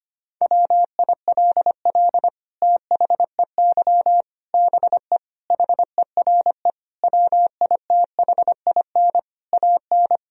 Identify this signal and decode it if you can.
Morse code, heard as a keyed tone, about 25 wpm, 700 Hz. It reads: WILL THEY BE HERE WITHIN AN
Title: Sentences from Top 1000 Words - Review Part 4 - 25wpm